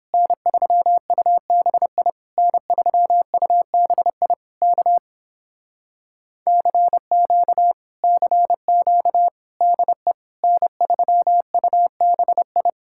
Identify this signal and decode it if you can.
Morse code, heard as a keyed tone, about 31 wpm, 700 Hz. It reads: N3UBI N3UBI K CQ CQ DE N3UBI